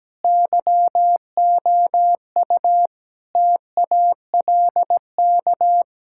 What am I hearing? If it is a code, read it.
Morse code, heard as a keyed tone, about 17 wpm, 700 Hz. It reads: YOU TALK
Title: Sentences from Top 1000 Words - Review Part 6 - 17wpm